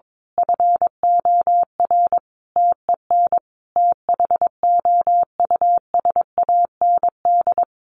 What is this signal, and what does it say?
Morse code, heard as a keyed tone, about 22 wpm, 700 Hz. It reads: FOR TEN THOUSAND
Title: A Sound Of Thunder 22wpm